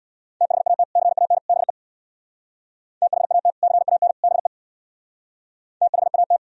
Morse code, heard as a keyed tone, about 65 wpm, 700 Hz. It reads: THAT CAN BE THAT CAN BE THAT